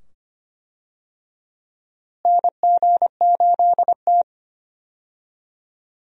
Morse code, heard as a keyed tone, about 25 wpm, 700 Hz. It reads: NG8T
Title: USA Call Signs - 4 Characters - Repeated 2x - 25wpm